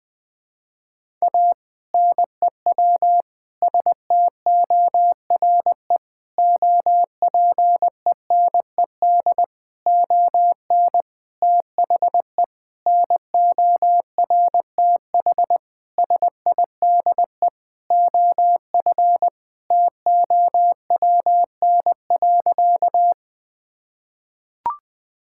Morse code, heard as a keyed tone, about 20 wpm, 700 Hz. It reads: A NEW STORE OPENED ON THE NORTH SIDE OF TOWN.
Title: Sentences with 6-Word Phrases - Spoken Beginning - 20wpm